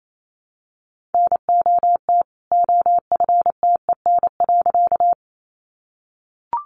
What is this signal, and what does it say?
Morse code, heard as a keyed tone, about 28 wpm, 700 Hz. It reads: NOT OFTEN.
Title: A Sound Of Thunder 28wpm